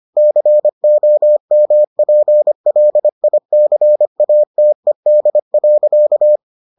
Morse code, heard as a keyed tone, about 25 wpm, 600 Hz. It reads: COMPLICATED.